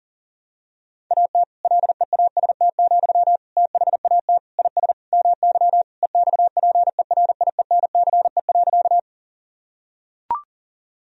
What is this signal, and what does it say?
Morse code, heard as a keyed tone, about 40 wpm, 700 Hz. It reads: AT LEAST, THAT IS MY EXPERIENCE.
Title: Sentences from Top 1000 Words - Review Part 3 - 40wpm